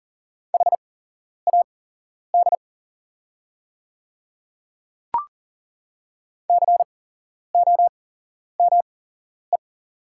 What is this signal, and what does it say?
Morse code, heard as a keyed tone, about 39 wpm, spaced as for 11 wpm, 690 Hz. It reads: HAD COME
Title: 8x Character Spacing - Top 300 Words - 40wpm